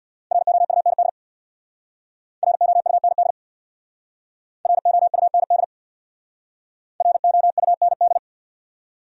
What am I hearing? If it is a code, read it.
Morse code, heard as a keyed tone, about 64 wpm, 700 Hz. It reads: ROUND ROUND ROUND ROUND